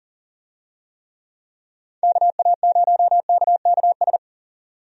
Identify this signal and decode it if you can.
Morse code, heard as a keyed tone, about 40 wpm, 700 Hz. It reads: KA0KKS
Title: USA Call Signs - 6 Characters - 40wpm